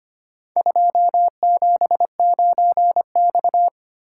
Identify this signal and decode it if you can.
Morse code, heard as a keyed tone, about 25 wpm, 700 Hz. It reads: 279X
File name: practice / 25wpm / Four Letter-Number Combinations - 25wpm